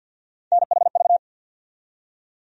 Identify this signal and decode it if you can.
Morse code, heard as a keyed tone, about 51 wpm, 700 Hz. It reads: NHV
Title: USA Call Sign Postfixes - 3 Letters 50wpm